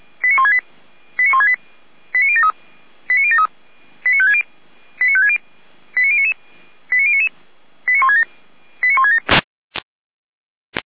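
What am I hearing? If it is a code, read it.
Five-tone selective call (ZVEI), heard as quick runs of five tones, tones of about 70 ms. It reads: 89168 89168 89073 89073 89560 89560 89099 89099 89167 89167
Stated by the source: ZVEI five-tone selective call 89220 received with an SDR, fire brigade Feuerwehr Orlamünde.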